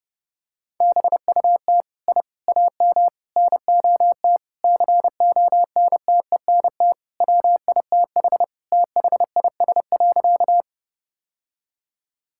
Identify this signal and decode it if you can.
Morse code, heard as a keyed tone, about 30 wpm, 700 Hz. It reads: BUT I AM NOT CONTENT WITH THIS.
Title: Sentences from Top 1000 Words - Review Part 6 - 30wpm